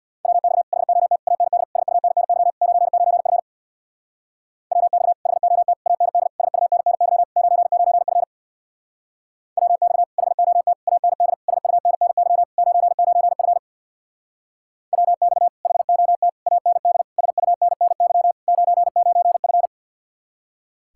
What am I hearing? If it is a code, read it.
Morse code, heard as a keyed tone, about 64 wpm, 700 Hz. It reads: WX HOT AND SUNNY 99F WX HOT AND SUNNY 99F WX HOT AND SUNNY 99F WX HOT AND SUNNY 99F